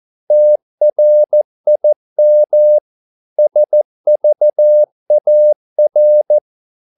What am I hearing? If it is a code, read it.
Morse code, heard as a keyed tone, about 14 wpm, 600 Hz. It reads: TRIM SVAR
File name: Korte ord DK 14wpm